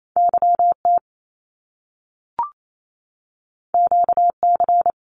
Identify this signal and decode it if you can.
Morse code, heard as a keyed tone, about 28 wpm, 700 Hz. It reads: YT QC
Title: Canadian Provinces and Territories 28wpm